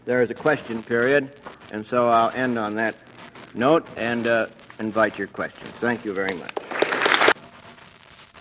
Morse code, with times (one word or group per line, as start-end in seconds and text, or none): none